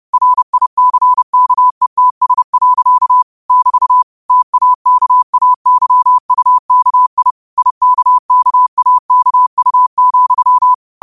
Morse code, heard as a keyed tone, about 30 wpm, 1000 Hz. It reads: RICKETS. <BT> TAKAYUKI IKKAKU,
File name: qotdfiles / 30wpm